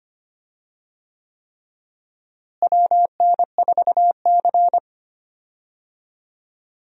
Morse code, heard as a keyed tone, about 25 wpm, 700 Hz. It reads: WN4C
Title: USA Call Signs - 4 Characters - Repeated 3x - 25wpm